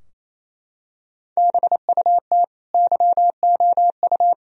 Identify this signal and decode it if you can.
Morse code, heard as a keyed tone, about 28 wpm, 700 Hz. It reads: BUT YOU
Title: Sentences from Top 1000 Words - Review Part 3 - 28wpm